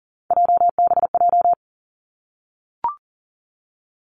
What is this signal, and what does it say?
Morse code, heard as a keyed tone, about 39 wpm, 700 Hz. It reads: JBJ
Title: Three Challenging Letter-Number Combinations 40wpm